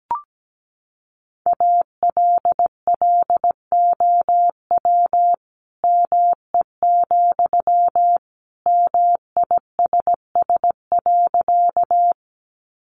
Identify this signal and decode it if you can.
Morse code, heard as a keyed tone, about 17 wpm, 700 Hz. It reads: ALLOW ME, MISS.
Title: Sentences from Top 700 Words 17wpm